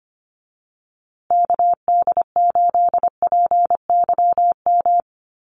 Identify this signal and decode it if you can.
Morse code, heard as a keyed tone, about 25 wpm, 700 Hz. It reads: KD8PYM
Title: USA Call Signs Repeated 2x 25wpm